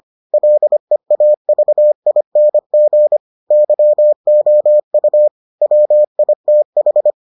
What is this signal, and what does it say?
Morse code, heard as a keyed tone, about 25 wpm, 600 Hz. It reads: LEAVING YOU WITH